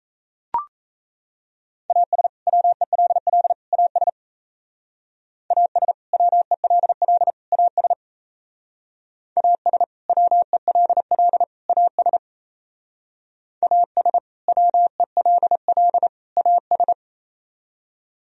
Morse code, heard as a keyed tone, about 36 wpm, 700 Hz. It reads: AS WELL AS AS WELL AS AS WELL AS AS WELL AS